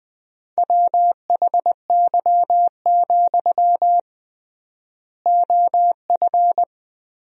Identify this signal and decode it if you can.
Morse code, heard as a keyed tone, about 21 wpm, 700 Hz. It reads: WHY, OF
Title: 3x Word-Spacing - Sentences from Top 500 Words - 20wpm